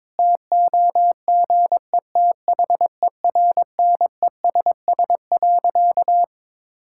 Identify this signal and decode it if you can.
Morse code, heard as a keyed tone, about 22 wpm, 700 Hz. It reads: TOGETHERNESS.
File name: Difficult Sentences with Hints - 22wpm